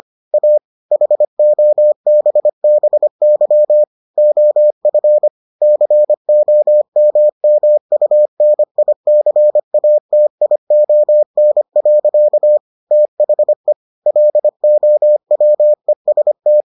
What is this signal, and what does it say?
Morse code, heard as a keyed tone, about 25 wpm, 600 Hz. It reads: A HOBBY OF COMMUNICATION. THE LOWEST